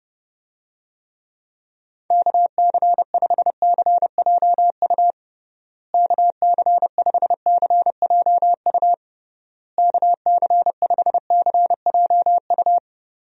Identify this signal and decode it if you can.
Morse code, heard as a keyed tone, about 30 wpm, 700 Hz. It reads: KC5CJU KC5CJU KC5CJU
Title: USA Call Signs - 6 Characters - Repeated 3x - 30wpm